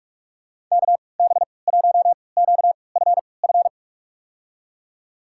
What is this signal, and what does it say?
Morse code, heard as a keyed tone, about 41 wpm, spaced as for 25 wpm, 700 Hz. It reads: KB1QFF
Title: USA Call Signs - 6 Characters - 3x Character Spacing - 45wpm